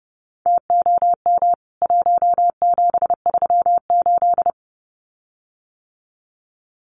Morse code, heard as a keyed tone, about 30 wpm, 700 Hz. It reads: TOM 1738
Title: CWT Exchange 30wpm